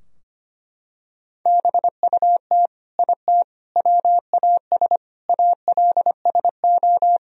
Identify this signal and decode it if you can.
Morse code, heard as a keyed tone, about 25 wpm, 700 Hz. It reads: BUT IT WAS ALSO
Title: Sentences from Top 1000 Words - Review Part 2 - 25wpm